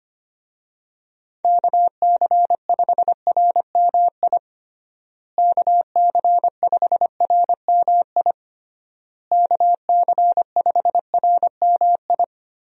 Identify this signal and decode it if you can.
Morse code, heard as a keyed tone, about 25 wpm, 700 Hz. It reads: KC5RMI KC5RMI KC5RMI
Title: USA Call Signs - 6 Characters - Repeated 3x - 25wpm